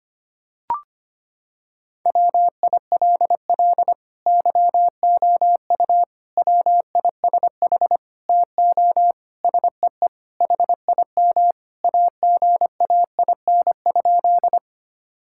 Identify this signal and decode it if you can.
Morse code, heard as a keyed tone, about 25 wpm, 700 Hz. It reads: WILL YOU WISH TO SEE HIM AGAIN?
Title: Sentences from Top 1600 Words 25wpm